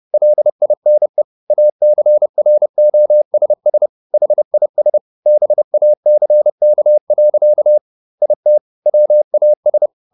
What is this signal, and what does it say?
Morse code, heard as a keyed tone, about 30 wpm, 600 Hz. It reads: LINE ACROSS HIS BACK. IT WAS
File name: the old man and the sea / the old man and the sea 1 30wpm